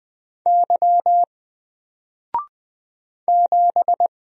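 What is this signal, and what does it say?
Morse code, heard as a keyed tone, about 20 wpm, 700 Hz. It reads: Y 7
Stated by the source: Single letter and number Morse drill at 20 wpm, from N4LCD.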